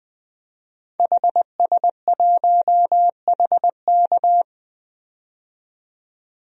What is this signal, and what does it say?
Morse code, heard as a keyed tone, about 20 wpm, 700 Hz. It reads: HS1HK
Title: DXCC-100 Call Signs - 5 Characters - 20wpm